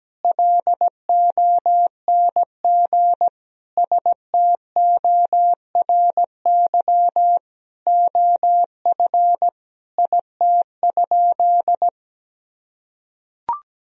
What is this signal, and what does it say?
Morse code, heard as a keyed tone, about 17 wpm, 700 Hz. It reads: LONG STORY OF IT?
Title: Sentences from Top 1000 Words - Review Part 1 - 17wpm